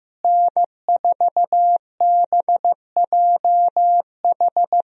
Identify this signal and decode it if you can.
Morse code, heard as a keyed tone, about 15 wpm, 700 Hz. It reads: N4BJH
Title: USA Call Signs - 5 Characters - Repeated 3x - 15wpm